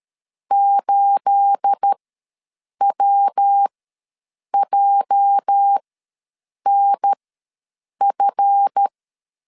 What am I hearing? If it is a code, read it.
Morse code, heard as a keyed tone, about 13 wpm, spaced as for 7 wpm, 800 Hz. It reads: 8WJNF
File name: LEZIONI / 10 SIGNS / 10C DOT ws 12wpm 3548